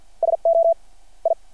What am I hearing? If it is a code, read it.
Morse code, heard as a keyed tone, about 51 wpm, spaced as for 44 wpm, 670 Hz. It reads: SO I
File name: archived / CRSnet